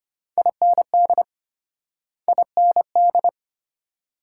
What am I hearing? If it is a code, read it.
Morse code, heard as a keyed tone, about 33 wpm, 700 Hz. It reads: IND IND